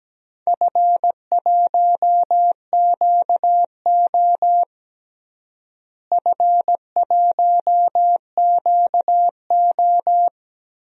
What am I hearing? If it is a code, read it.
Morse code, heard as a keyed tone, about 17 wpm, 700 Hz. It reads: F1QO F1QO
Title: DXCC-100 Call Signs - 4 Characters - Repeated 2x - 17wpm